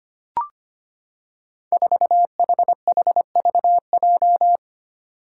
Morse code, heard as a keyed tone, about 25 wpm, 700 Hz. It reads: 4HHVJ
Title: Five Challenging Letter-Number Combinations 25wpm